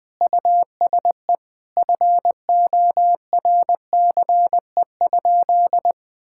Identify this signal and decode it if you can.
Morse code, heard as a keyed tone, about 20 wpm, 700 Hz. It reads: USE FORCE?
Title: Sentences from Top 1600 Words 20wpm